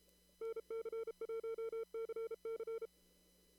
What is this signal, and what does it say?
Morse code, heard as a keyed tone, about 33 wpm, 450 Hz. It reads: NC1CC
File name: CW / kt5j wpx cw-op-k5pi